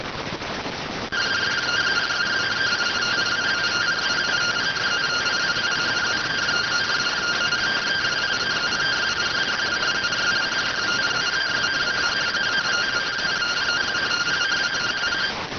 Two-tone teleprinter signal, about 170 Hz shift.